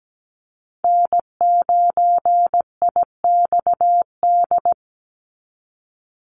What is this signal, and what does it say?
Morse code, heard as a keyed tone, about 17 wpm, 700 Hz. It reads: N9IXD
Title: Call Signs Repeated 2x - 17wpm